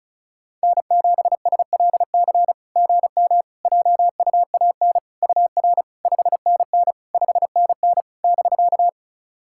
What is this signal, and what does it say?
Morse code, heard as a keyed tone, about 36 wpm, 700 Hz. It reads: N7SLC GM JUAN UR 5NN 5NN <BK>
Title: SOTA QSO Practice 35wpm